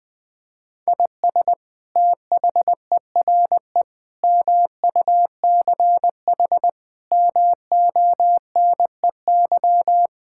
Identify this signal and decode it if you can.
Morse code, heard as a keyed tone, about 20 wpm, 700 Hz. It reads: IS THERE MUCH MONEY
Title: Sentences from Top 1000 Words - Review Part 5 - 20wpm